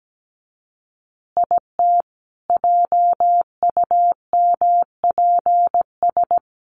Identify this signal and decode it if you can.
Morse code, heard as a keyed tone, about 17 wpm, 700 Hz. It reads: IT JUMPS
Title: Additional Easy Phrases 17wpm